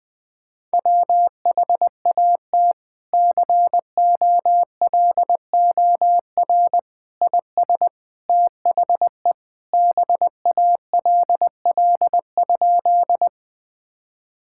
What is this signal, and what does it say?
Morse code, heard as a keyed tone, about 21 wpm, 700 Hz. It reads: WHAT COLOR IS THE BALL?
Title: Sentences from Top 1000 Words - Review Part 6 - 20wpm